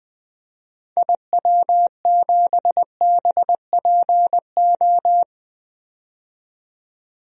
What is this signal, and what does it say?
Morse code, heard as a keyed tone, about 21 wpm, 700 Hz. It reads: IW7BPO
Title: DXCC-100 Call Signs - 6 Characters - 20wpm